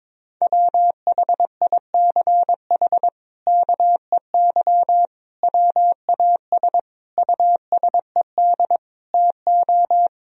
Morse code, heard as a keyed tone, about 21 wpm, 700 Hz. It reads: WHICH KEY WAS USED TO
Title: Sentences from Top 1000 Words - Review Part 4 - 22wpm